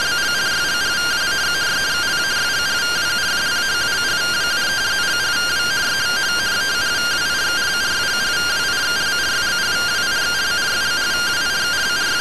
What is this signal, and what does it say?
Two-tone teleprinter signal, about 170 Hz shift.